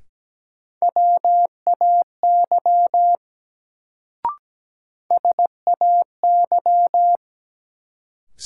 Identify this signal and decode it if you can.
Morse code, heard as a keyed tone, about 17 wpm, 700 Hz. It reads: WAY SAY
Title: Sets of 2 Words - New Word Drill - 17wpm